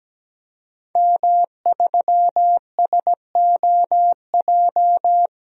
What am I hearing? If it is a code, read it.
Morse code, heard as a keyed tone, about 17 wpm, 700 Hz. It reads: M3SOJ
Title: DXCC-100 Call Signs - 5 Characters - 17wpm